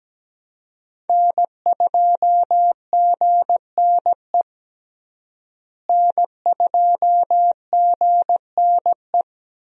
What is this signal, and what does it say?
Morse code, heard as a keyed tone, about 17 wpm, 700 Hz. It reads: N2GNE N2GNE
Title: USA Call Signs - 5 Characters - Repeated 3x - 17wpm